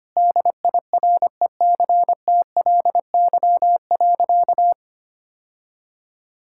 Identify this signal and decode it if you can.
Morse code, heard as a keyed tone, about 25 wpm, 700 Hz. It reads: DIRECTLY.
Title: Sentences from Top 1000 Words - Review Part 5 - 25wpm